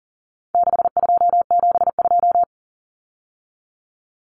Morse code, heard as a keyed tone, about 40 wpm, 700 Hz. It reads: 6272
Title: Common Cut Numbers - 4-Digit 40wpm